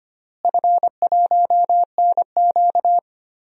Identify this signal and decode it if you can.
Morse code, heard as a keyed tone, about 25 wpm, 700 Hz. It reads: F1NQ
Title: DXCC-100 Call Signs - 4 Characters - Repeated 2x - 25wpm